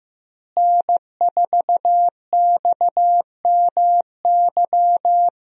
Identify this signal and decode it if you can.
Morse code, heard as a keyed tone, about 15 wpm, 700 Hz. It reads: N4XMY
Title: USA Call Signs - 5 Characters - Repeated 3x - 15wpm